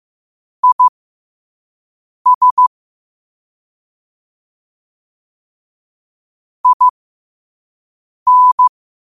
Morse code, heard as a keyed tone, about 15 wpm, spaced as for 5 wpm, 1000 Hz. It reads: IS IN
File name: qotdfiles / 5wpm